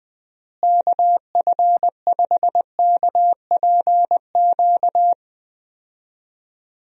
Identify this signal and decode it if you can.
Morse code, heard as a keyed tone, about 20 wpm, 700 Hz. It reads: KF5KPQ
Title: USA Call Signs - 6 Characters - Repeated 2x - 20wpm